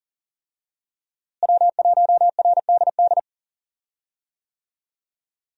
Morse code, heard as a keyed tone, about 40 wpm, 700 Hz. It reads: W1RDD
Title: USA Call Signs - 5 Characters - Repeated 3x - 40wpm